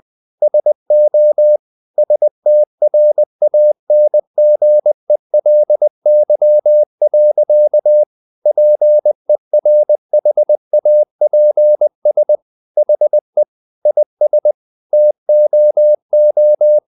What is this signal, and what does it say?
Morse code, heard as a keyed tone, about 20 wpm, 600 Hz. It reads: SO STRANGELY. PERHAPS HE IS TOO
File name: the old man and the sea / the old man and the sea part 1 20wpm